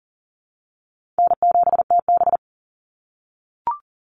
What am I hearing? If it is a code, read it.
Morse code, heard as a keyed tone, about 37 wpm, 700 Hz. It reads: N7TB
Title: Call Signs - 40wpm